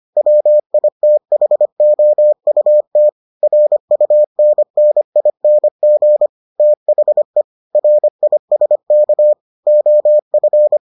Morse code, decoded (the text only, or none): WITHOUT RUNNING THE RISK OF